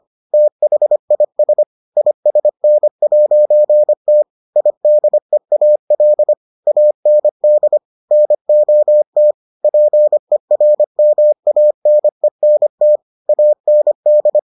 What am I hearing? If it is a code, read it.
Morse code, heard as a keyed tone, about 25 wpm, 600 Hz. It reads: THIS ISN'T IDEAL AND NOT PERMANENT AND